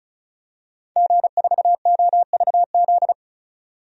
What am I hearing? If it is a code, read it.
Morse code, heard as a keyed tone, about 34 wpm, 700 Hz. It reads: G4OVZ
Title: DXCC-100 Call Signs - 5 Characters - 35wpm